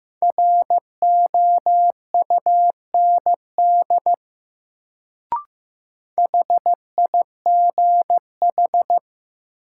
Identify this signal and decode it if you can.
Morse code, heard as a keyed tone, about 15 wpm, 700 Hz. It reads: ROUND HIGH
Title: Sentences from Top 200 Words - New Word Drill - 15wpm